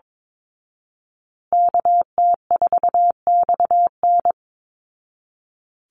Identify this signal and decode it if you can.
Morse code, heard as a keyed tone, about 22 wpm, 700 Hz. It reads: KT4XN
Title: Call Signs Repeated 3x - 22wpm